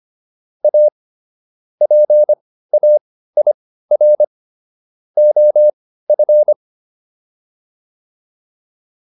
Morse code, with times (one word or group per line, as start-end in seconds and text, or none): 0.6-0.9 A
1.8-4.2 PAIR
5.2-6.5 OF